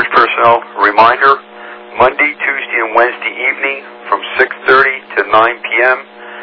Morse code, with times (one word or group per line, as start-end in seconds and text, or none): none